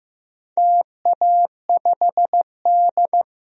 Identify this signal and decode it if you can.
Morse code, heard as a keyed tone, about 15 wpm, 700 Hz. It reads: TA5D
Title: DXCC-100 Call Signs - 4 Characters - 15wpm